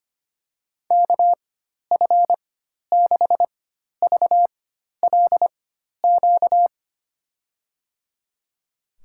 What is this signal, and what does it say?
Morse code, heard as a keyed tone, about 25 wpm, spaced as for 12 wpm, 700 Hz. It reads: KF6VLQ
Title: USA Call Signs - 6 Characters - 4x Character Spacing - 25wpm